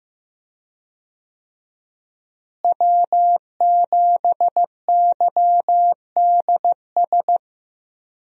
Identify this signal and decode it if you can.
Morse code, heard as a keyed tone, about 15 wpm, 700 Hz. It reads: W7YDS
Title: USA Call Signs - 5 Characters - Repeated 3x - 15wpm